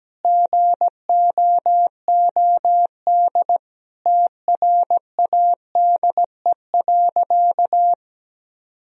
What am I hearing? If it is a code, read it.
Morse code, heard as a keyed tone, about 17 wpm, 700 Hz. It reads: GOOD TRADE.
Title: Sentences from Top 1000 Words - Review Part 4 - 17wpm